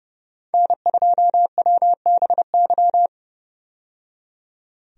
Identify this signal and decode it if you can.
Morse code, heard as a keyed tone, about 32 wpm, 700 Hz. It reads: N2WBY
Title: USA Call Signs - 5 Characters - 30wpm